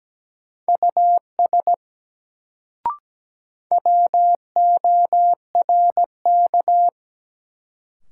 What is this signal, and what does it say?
Morse code, heard as a keyed tone, about 17 wpm, 700 Hz. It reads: US WORK
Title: Sentences from Top 200 Words - New Word Drill - 17wpm